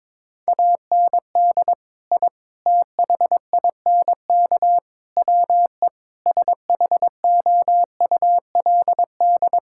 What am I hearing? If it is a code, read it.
Morse code, heard as a keyed tone, about 22 wpm, 700 Hz. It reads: AND I THINK WE SHOULD
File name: practice / 22wpm / Sentences with 4-Word Phrases - 22wpm